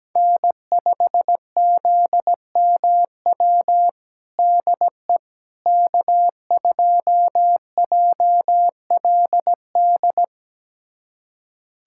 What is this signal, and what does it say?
Morse code, heard as a keyed tone, about 17 wpm, 700 Hz. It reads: N5ZMW DE K2JLD
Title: USA Callsign DE Callsign - 5 Characters - 17wpm